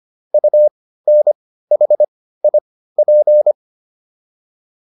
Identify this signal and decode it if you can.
Morse code, heard as a keyed tone, about 25 wpm, spaced as for 15 wpm, 600 Hz. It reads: UNHIP